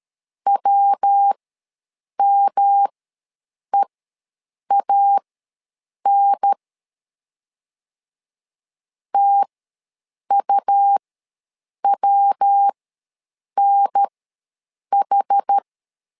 Morse code, heard as a keyed tone, about 13 wpm, spaced as for 7 wpm, 800 Hz. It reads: WMEAN TUWNH
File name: LEZIONI / 3 MWRO / 3 MW ws 12wpm 1816